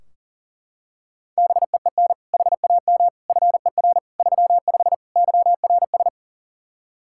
Morse code, heard as a keyed tone, about 40 wpm, 700 Hz. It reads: BEEN HAM FER 35 YRS
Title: Practice QSO - Exchange 3 - More Info - 40wpm